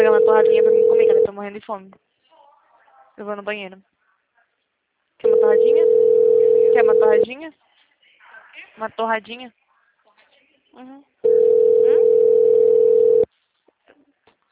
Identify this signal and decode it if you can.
Ringback tone (North America), 440 + 480 Hz, with speech.
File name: out- -2012- - - .2010